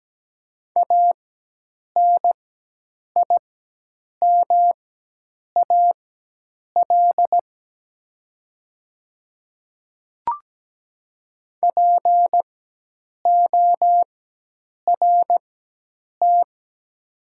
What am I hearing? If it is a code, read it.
Morse code, heard as a keyed tone, about 17 wpm, spaced as for 8 wpm, 690 Hz. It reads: ANIMAL PORT
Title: 4x Character Spacing - Top 200 Words - 17wpm